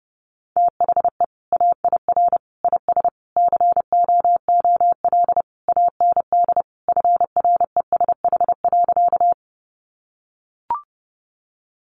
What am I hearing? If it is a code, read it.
Morse code, heard as a keyed tone, about 30 wpm, 700 Hz. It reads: THE AIR IS COOL AND FRESH.
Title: Sentences from Top 1200 Words 30wpm